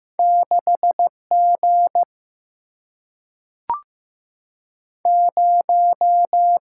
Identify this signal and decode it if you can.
Morse code, heard as a keyed tone, about 15 wpm, 700 Hz. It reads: 6 G 0
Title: Single-Letter Single-Number 15wpm